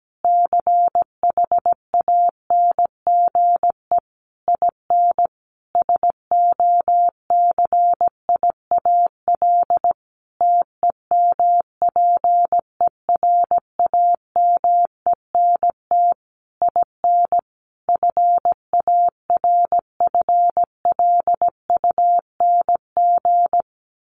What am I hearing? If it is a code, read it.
Morse code, heard as a keyed tone, about 17 wpm, 700 Hz. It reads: CHANGE IN SOCIAL TEMPERAMENT IN FARFLUNG
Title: A Sound Of Thunder 17wpm